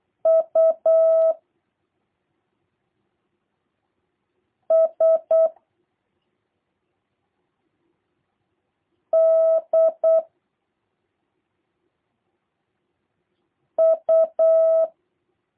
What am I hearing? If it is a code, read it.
Morse code, heard as a keyed tone, about 8 wpm, 650 Hz. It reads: U S D U
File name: jh8cbh / 2023cw / review1-3